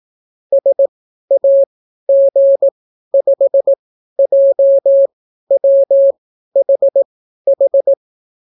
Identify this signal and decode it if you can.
Morse code, heard as a keyed tone, about 18 wpm, spaced as for 12 wpm, 550 Hz. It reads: SAG5JWHH